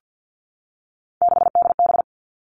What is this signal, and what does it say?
Morse code, heard as a keyed tone, about 50 wpm, 700 Hz. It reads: 6DB
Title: Three Challenging Letter-Number Combinations 50wpm